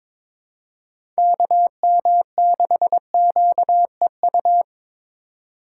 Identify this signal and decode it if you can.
Morse code, heard as a keyed tone, about 22 wpm, 700 Hz. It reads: KM6QEU
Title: USA Call Signs - 6 Characters - 3x Character Spacing - 22wpm